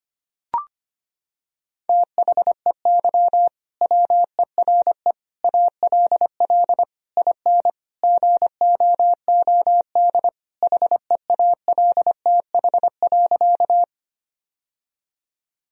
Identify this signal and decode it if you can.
Morse code, heard as a keyed tone, about 25 wpm, 700 Hz. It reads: THEY WERE ALL IN GOOD HEALTH.
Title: Sentences from Top 1000 Words - Review Part 3 - 25wpm